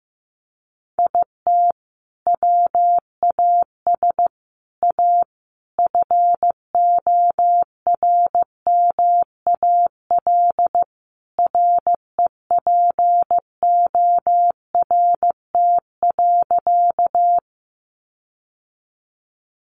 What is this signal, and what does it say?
Morse code, heard as a keyed tone, about 15 wpm, 700 Hz. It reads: IT WAS A FORMAL REPORT.
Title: Sentences from Top 1300 Words 15wpm